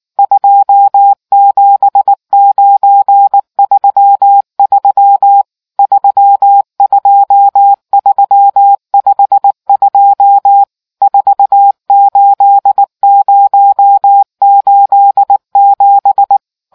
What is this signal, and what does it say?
Morse code, heard as a keyed tone, about 19 wpm, 800 Hz. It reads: 27933 32352 48087